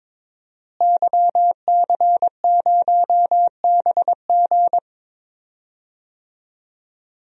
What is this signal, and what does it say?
Morse code, heard as a keyed tone, about 22 wpm, 700 Hz. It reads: YC0BG
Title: DXCC-100 Call Signs - 5 Characters - Repeated 2x - 22wpm